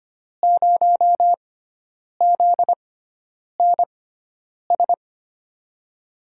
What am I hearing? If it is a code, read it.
Morse code, heard as a keyed tone, about 25 wpm, spaced as for 9 wpm, 700 Hz. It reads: 0 ZNS
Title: USA Call Signs - 5 Characters - 6x Character Spacing - 25wpm